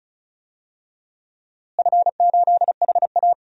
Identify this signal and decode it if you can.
Morse code, heard as a keyed tone, about 36 wpm, 700 Hz. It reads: F8HA
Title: DXCC-100 Call Signs - 4 Characters - Repeated 2x - 35wpm